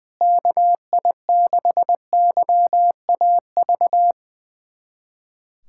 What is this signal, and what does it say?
Morse code, heard as a keyed tone, about 21 wpm, 700 Hz. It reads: KI6YAV
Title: USA Callsign DE Callsign - 6 Characters - 20wpm